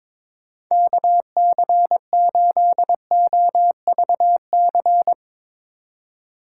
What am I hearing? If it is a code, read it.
Morse code, heard as a keyed tone, about 22 wpm, 700 Hz. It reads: KC8OVC
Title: USA Call Signs - 6 Characters - 4x Character Spacing - 22wpm